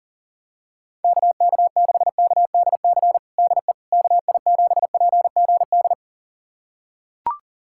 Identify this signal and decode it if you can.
Morse code, heard as a keyed tone, about 40 wpm, 700 Hz. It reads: KK6KDC DE KI7PGD
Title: USA Callsign DE Callsign 40wpm